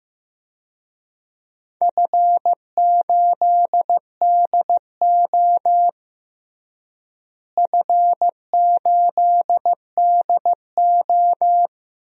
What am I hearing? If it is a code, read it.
Morse code, heard as a keyed tone, about 15 wpm, 700 Hz. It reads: F8DO F8DO
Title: DXCC-100 Call Signs - 4 Characters - Repeated 3x - 15wpm